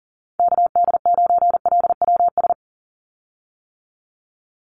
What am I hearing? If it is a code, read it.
Morse code, heard as a keyed tone, about 39 wpm, 700 Hz. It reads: KD9LWS
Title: USA Call Signs 40wpm